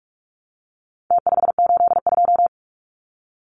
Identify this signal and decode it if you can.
Morse code, heard as a keyed tone, about 44 wpm, 700 Hz. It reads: T582
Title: Numbers - Contest Style - 4-Digit 45wpm